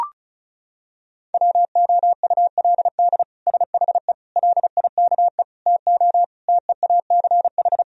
Morse code, heard as a keyed tone, about 34 wpm, 700 Hz. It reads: WOULD SHE LIKE TO TEACH
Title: Sentences from Top 1000 Words - Review Part 2 - 35wpm